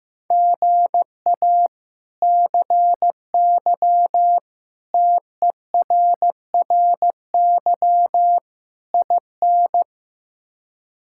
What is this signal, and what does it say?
Morse code, heard as a keyed tone, about 15 wpm, 700 Hz. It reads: GA CY TERRY IN
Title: K1USN SST QSO Practice - 15wpm